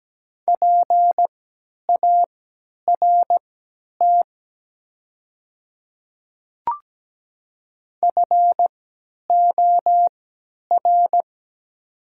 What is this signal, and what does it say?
Morse code, heard as a keyed tone, about 17 wpm, spaced as for 10 wpm, 700 Hz. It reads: PART FOR
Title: 3x Character Spacing - Top 500 Words - 17wpm